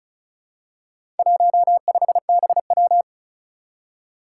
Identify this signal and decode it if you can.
Morse code, heard as a keyed tone, about 36 wpm, 700 Hz. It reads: 15BW
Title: Four Challenging Letter-Number Combinations 35wpm